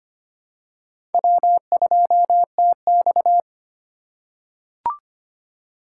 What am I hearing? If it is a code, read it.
Morse code, heard as a keyed tone, about 25 wpm, 700 Hz. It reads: W2TX
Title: USA Call Signs - 4 Characters - 4x Character Spacing - 25wpm